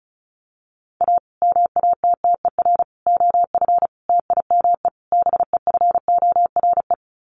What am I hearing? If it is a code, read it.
Morse code, heard as a keyed tone, about 34 wpm, 700 Hz. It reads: A MATTER OF TIME BEFORE
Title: Sets of 5 Words Encore 35wpm